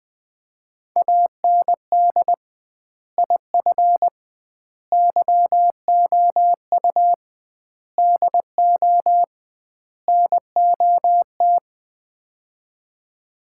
Morse code, heard as a keyed tone, about 20 wpm, 700 Hz. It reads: AND IF YOU DO NOT
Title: 2x Word-Spacing - Sets of 5 Words - 20wpm